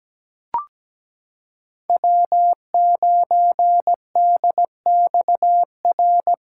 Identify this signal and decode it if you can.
Morse code, heard as a keyed tone, about 17 wpm, 700 Hz. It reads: W9DXR
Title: K1USN SST QSO Practice - 17wpm